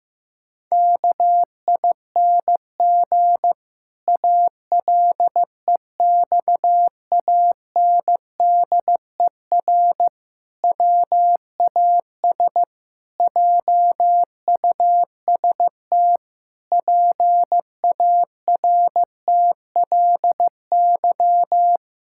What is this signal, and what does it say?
Morse code, heard as a keyed tone, about 15 wpm, 700 Hz. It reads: KING ALEXANDER WAS JUST PARTLY
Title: Pangram Challenge v2 - 15wpm